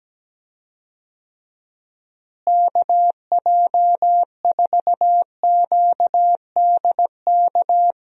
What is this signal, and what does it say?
Morse code, heard as a keyed tone, about 17 wpm, 700 Hz. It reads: KJ4QDK
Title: USA Call Signs - 6 Characters - Repeated 3x - 17wpm